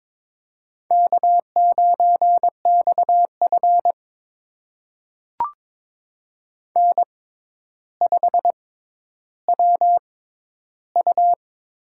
Morse code, heard as a keyed tone, about 22 wpm, spaced as for 11 wpm, 700 Hz. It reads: K9XF N5WU
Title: USA Call Signs - 4 Characters - 6x Character Spacing - 22wpm